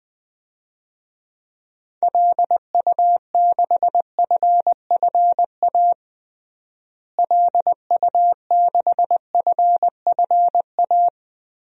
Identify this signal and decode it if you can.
Morse code, heard as a keyed tone, about 20 wpm, 700 Hz. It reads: LU6FFA LU6FFA
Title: DXCC-100 Call Signs - 6 Characters - Repeated 3x - 20wpm